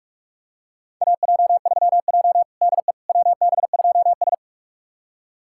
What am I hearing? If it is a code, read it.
Morse code, heard as a keyed tone, about 47 wpm, 700 Hz. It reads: AJ3J DE WB2S
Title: USA Callsign DE Callsign - 4 Characters - 45wpm